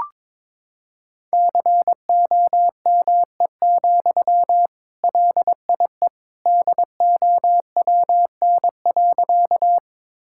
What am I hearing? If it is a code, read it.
Morse code, heard as a keyed tone, about 22 wpm, 700 Hz. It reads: COME, LIE DOWN.
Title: Sentences from Top 1600 Words 22wpm